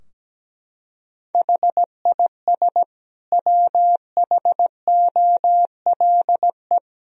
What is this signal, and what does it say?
Morse code, heard as a keyed tone, about 17 wpm, 700 Hz. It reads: HIS WHOLE
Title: Sentences from Top 1000 Words - Review Part 3 - 17wpm